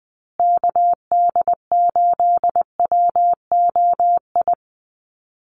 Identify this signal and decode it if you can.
Morse code, heard as a keyed tone, about 20 wpm, 700 Hz. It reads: KD8WOI
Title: USA Call Signs Repeated 2x 20wpm